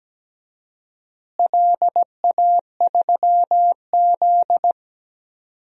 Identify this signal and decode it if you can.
Morse code, heard as a keyed tone, about 17 wpm, 700 Hz. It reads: LA3Z
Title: DXCC-100 Call Signs - 4 Characters - Repeated 3x - 17wpm